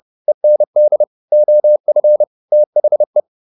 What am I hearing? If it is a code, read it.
Morse code, heard as a keyed tone, about 30 wpm, 600 Hz. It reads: END OF THE